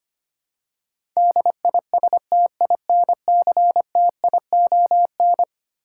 Morse code, heard as a keyed tone, about 25 wpm, 700 Hz. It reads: DISTINCTION
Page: morse-code-ninja-practice-25wpm